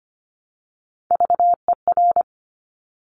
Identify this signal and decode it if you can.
Morse code, heard as a keyed tone, about 25 wpm, 700 Hz. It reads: VER
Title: Top Three Letter Combinations in Words 25wpm